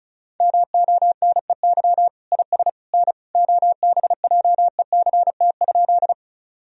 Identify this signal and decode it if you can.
Morse code, heard as a keyed tone, about 34 wpm, 700 Hz. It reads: MONEY IS N OBJECT?
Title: Sentences from Top 500 Words and Abbreviations 35wpm